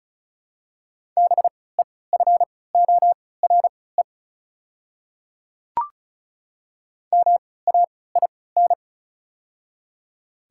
Morse code, heard as a keyed tone, about 34 wpm, spaced as for 20 wpm, 690 Hz. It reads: BEFORE MAIN
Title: 3x Character Spacing - Top 300 Words - 35wpm